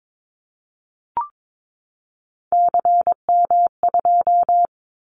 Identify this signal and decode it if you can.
Morse code, heard as a keyed tone, about 22 wpm, 700 Hz. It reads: CM2
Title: Callsign Prefixes for Top 100 DXCC - 22wpm